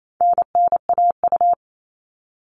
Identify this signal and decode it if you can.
Morse code, heard as a keyed tone, about 28 wpm, 700 Hz. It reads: NNAU
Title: All Cut Numbers - 4-Digit 28wpm